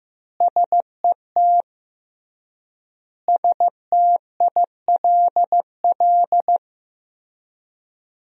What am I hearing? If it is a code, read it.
Morse code, heard as a keyed tone, about 15 wpm, 700 Hz. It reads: SET STILL
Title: 3x Word-Spacing - Sentences from Top 300 Words - 15wpm